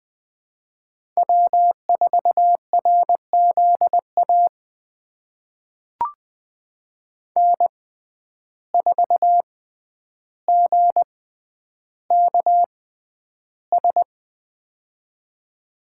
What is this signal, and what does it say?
Morse code, heard as a keyed tone, about 20 wpm, spaced as for 10 wpm, 700 Hz. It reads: W4RZA N4GKS
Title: USA Call Signs - 5 Characters - 6x Character Spacing - 20wpm